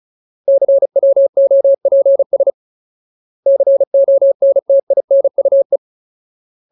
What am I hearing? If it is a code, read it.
Morse code, heard as a keyed tone, about 34 wpm, 550 Hz. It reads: CWOPS CONTINUE
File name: Ultimate Word List / 35WPM Ultimate